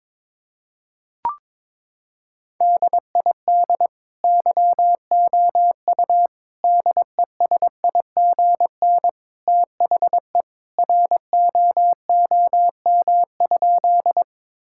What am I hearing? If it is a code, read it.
Morse code, heard as a keyed tone, about 22 wpm, 700 Hz. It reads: DID YOU DESIGN THE ROOM?
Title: Sentences from Top 1000 Words - Review Part 5 - 22wpm